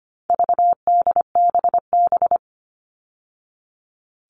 Morse code, heard as a keyed tone, about 25 wpm, 700 Hz. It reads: VDBB
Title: All Cut Numbers - 4-Digit 25wpm